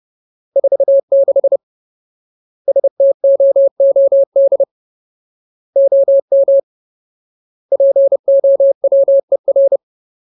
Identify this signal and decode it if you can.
Morse code, heard as a keyed tone, about 30 wpm, 550 Hz. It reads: 46 STOOD OM POWER
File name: Ultimate Word List / 30WPM Ultimate